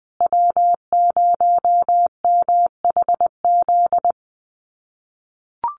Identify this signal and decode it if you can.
Morse code, heard as a keyed tone, about 20 wpm, 700 Hz. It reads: W0MHZ
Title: USA Call Signs Repeated 3x 20wpm